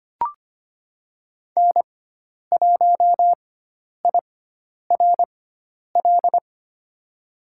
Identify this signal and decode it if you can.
Morse code, heard as a keyed tone, about 25 wpm, spaced as for 10 wpm, 700 Hz. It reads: N1IRL
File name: Lesson 16g - Letter c - Call Signs 25wpm